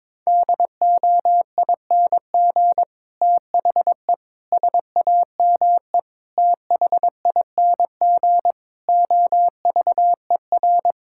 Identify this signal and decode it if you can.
Morse code, heard as a keyed tone, about 21 wpm, 700 Hz. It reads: DOING THE SAME THING OVER
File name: practice / 22wpm / Sets of 8 Words - 22wpm